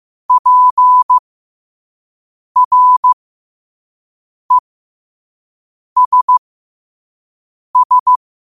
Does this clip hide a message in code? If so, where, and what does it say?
Yes: Morse code from 0.3 to 8.2 s, heard as a keyed tone, about 15 wpm, spaced as for 5 wpm, 1000 Hz. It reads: PRESS